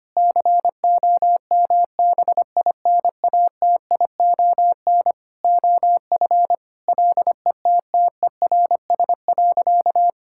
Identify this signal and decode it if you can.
Morse code, heard as a keyed tone, about 25 wpm, 700 Hz. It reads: COMBINATION OF LETTERS.
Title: Sentences from Top 1000 Words - Review Part 6 - 25wpm